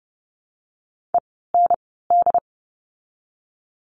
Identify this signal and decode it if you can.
Morse code, heard as a keyed tone, about 30 wpm, 700 Hz. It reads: E N D
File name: Lesson 16f - Letter c - Words 30wpm